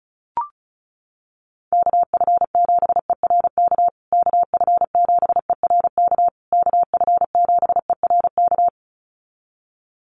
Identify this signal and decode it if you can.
Morse code, heard as a keyed tone, about 34 wpm, 700 Hz. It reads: KF7ERK KF7ERK KF7ERK
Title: USA Call Signs Repeated 3x 35wpm